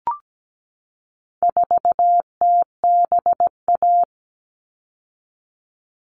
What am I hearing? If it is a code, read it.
Morse code, heard as a keyed tone, about 17 wpm, 700 Hz. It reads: 4TBA
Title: All Cut Numbers - 4-Digit 17wpm